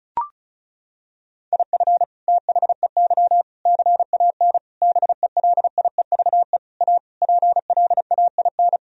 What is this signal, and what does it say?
Morse code, heard as a keyed tone, about 36 wpm, 700 Hz. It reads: IF THEY CAN BELIEVE A PLAIN
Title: Sentences from Top 1000 Words - Review Part 2 - 35wpm